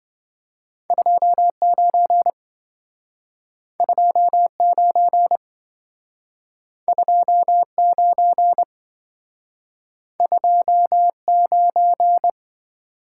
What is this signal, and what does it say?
Morse code, heard as a keyed tone, about 27 wpm, 700 Hz. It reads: 29 29 29 29